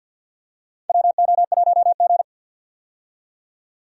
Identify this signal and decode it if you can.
Morse code, heard as a keyed tone, about 49 wpm, 700 Hz. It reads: WO1G
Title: USA Call Signs - 4 Characters - 50wpm